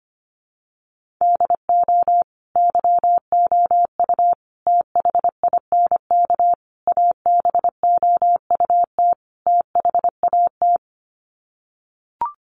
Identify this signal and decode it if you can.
Morse code, heard as a keyed tone, about 25 wpm, 700 Hz. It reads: DO YOU THINK ABOUT THAT
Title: Sets of 5 Words Encore2 25wpm